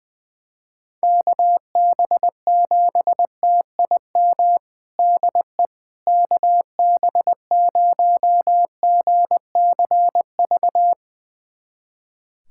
Morse code, heard as a keyed tone, about 21 wpm, 700 Hz. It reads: KB7TIM DE KB0GCV
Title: USA Callsign DE Callsign - 6 Characters - 20wpm